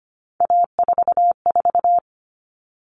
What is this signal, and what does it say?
Morse code, heard as a keyed tone, about 25 wpm, 700 Hz. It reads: A44
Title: Common Cut Numbers - 3-Digit 25wpm